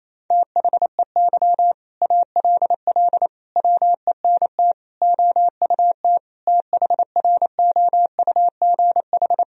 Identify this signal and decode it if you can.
Morse code, heard as a keyed tone, about 28 wpm, 700 Hz. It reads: THEY ALL WENT OUT THROUGH
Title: Sentences from Top 1000 Words - Review Part 1 - 28wpm